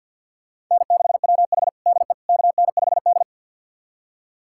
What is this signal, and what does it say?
Morse code, heard as a keyed tone, about 50 wpm, 700 Hz. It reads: N6WH DE KN5D
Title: USA Callsign DE Callsign - 4 Characters - 50wpm